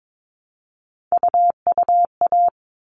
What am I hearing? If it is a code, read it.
Morse code, heard as a keyed tone, about 22 wpm, 700 Hz. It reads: UUA
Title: All Cut Numbers - 3-Digit 22wpm